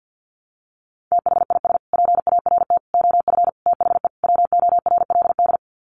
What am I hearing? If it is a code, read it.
Morse code, heard as a keyed tone, about 50 wpm, 700 Hz. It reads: THIS PART OF THE WORLD
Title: Sets of 5 Words Encore2 50wpm